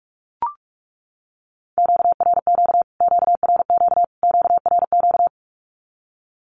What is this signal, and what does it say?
Morse code, heard as a keyed tone, about 44 wpm, 700 Hz. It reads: QRQ QRQ QRQ
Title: Q-Codes Repeated 3x 45wpm